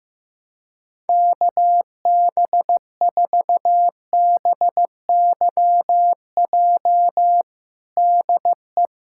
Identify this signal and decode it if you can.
Morse code, heard as a keyed tone, about 15 wpm, 700 Hz. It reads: KB4BYJ DE
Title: USA Callsign DE Callsign 15wpm